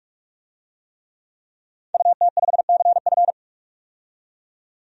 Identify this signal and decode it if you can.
Morse code, heard as a keyed tone, about 45 wpm, 700 Hz. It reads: UT5CF
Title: DXCC-100 Call Signs - 5 Characters - 45wpm